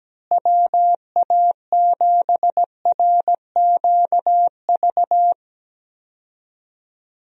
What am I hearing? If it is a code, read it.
Morse code, heard as a keyed tone, about 17 wpm, 700 Hz. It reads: WA7RQV
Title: USA Call Signs - 6 Characters - Repeated 3x - 17wpm